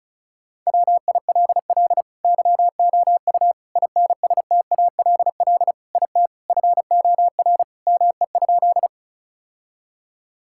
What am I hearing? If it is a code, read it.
Morse code, heard as a keyed tone, about 36 wpm, 700 Hz. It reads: WILL YOU INSTALL IT FOR ME?
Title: Sentences from Top 1600 Words 35wpm